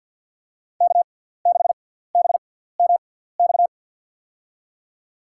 Morse code, heard as a keyed tone, about 53 wpm, spaced as for 17 wpm, 700 Hz. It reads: K6BMX
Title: USA Call Signs - 5 Characters - 6x Character Spacing - 50wpm